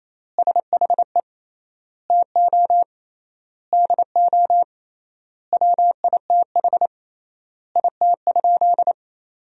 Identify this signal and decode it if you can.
Morse code, heard as a keyed tone, about 28 wpm, 700 Hz. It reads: SHE TO DO WITH IT?
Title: 3x Word-Spacing - Sentences from Top 100 Words - 28wpm